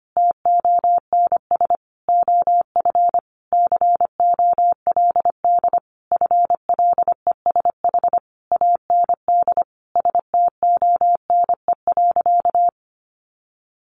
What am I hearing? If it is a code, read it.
Morse code, heard as a keyed tone, about 25 wpm, 700 Hz. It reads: TONS OF COLD FLESH AND STONE.
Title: A Sound Of Thunder 25wpm